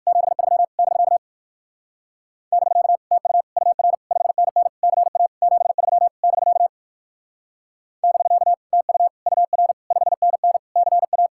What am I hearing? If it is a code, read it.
Morse code, heard as a keyed tone, about 63 wpm, 700 Hz. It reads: 73 <BK> <BK> TU UR 5NN CA 73 <BK> <BK> TU UR 5NN CA